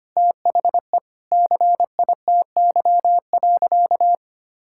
Morse code, heard as a keyed tone, about 25 wpm, 700 Hz. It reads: THE CITY.
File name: practice / 25wpm / Sentences with 4-Word Phrases - 25wpm